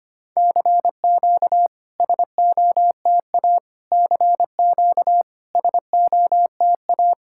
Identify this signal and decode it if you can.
Morse code, heard as a keyed tone, about 25 wpm, 700 Hz. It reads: CQ SOTA CQ SOTA
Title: SOTA QSO Practice 25wpm